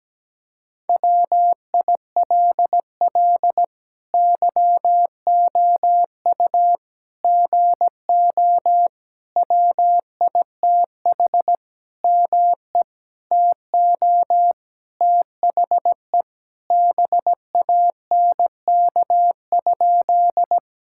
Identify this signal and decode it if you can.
Morse code, heard as a keyed tone, about 17 wpm, 700 Hz. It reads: WILL YOU GO WITH ME TO THE BANK?
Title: Sentences from Top 1000 Words - Review Part 6 - 17wpm